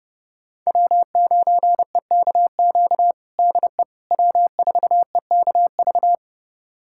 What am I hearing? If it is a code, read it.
Morse code, heard as a keyed tone, about 32 wpm, 700 Hz. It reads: W9EKQ DE W4EKV
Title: USA Callsign DE Callsign - 5 Characters - 30wpm